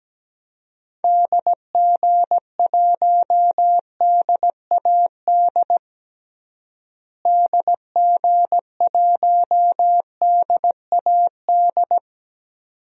Morse code, heard as a keyed tone, about 17 wpm, 700 Hz. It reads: DG1DAD DG1DAD
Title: DXCC-100 Call Signs - 6 Characters - Repeated 2x - 17wpm